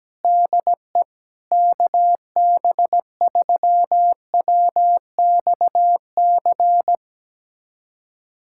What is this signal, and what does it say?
Morse code, heard as a keyed tone, about 17 wpm, 700 Hz. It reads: DE KB3WXC
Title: USA Callsign DE Callsign - 6 Characters - 17wpm